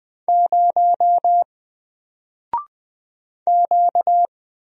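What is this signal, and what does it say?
Morse code, heard as a keyed tone, about 20 wpm, 700 Hz. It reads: 0 Q
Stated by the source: Single letter and number Morse drill at 20 wpm, from N4LCD.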